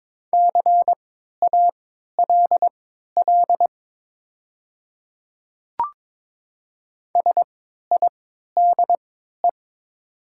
Morse code, heard as a keyed tone, about 22 wpm, spaced as for 13 wpm, 700 Hz. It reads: CALL SIDE
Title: 3x Character Spacing - Top 100 Words - 22wpm